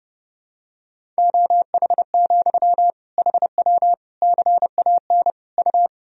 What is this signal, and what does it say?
Morse code, heard as a keyed tone, about 30 wpm, 700 Hz. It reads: OH, HW CAN U
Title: Sentences from Top 500 Words and Abbreviations 30wpm